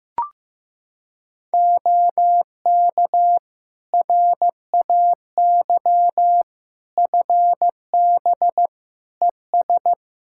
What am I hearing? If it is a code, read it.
Morse code, heard as a keyed tone, about 15 wpm, 700 Hz. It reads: OK RAY FB ES
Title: Practice QSO - Exchange 2 - Info - 15wpm